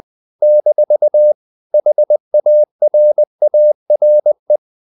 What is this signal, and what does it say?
Morse code, heard as a keyed tone, about 20 wpm, 600 Hz. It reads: - HARARE
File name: CW / Practice / NationalCapitals20